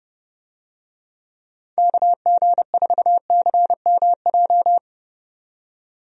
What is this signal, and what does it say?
Morse code, heard as a keyed tone, about 31 wpm, 700 Hz. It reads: KG4CMJ
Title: USA Call Signs - 6 Characters - 30wpm